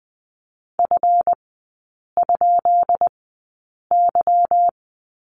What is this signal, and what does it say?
Morse code, heard as a keyed tone, about 20 wpm, spaced as for 8 wpm, 690 Hz. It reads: F?Y
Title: 145 Lesson 24e - Letter y - Three Characters 20wpm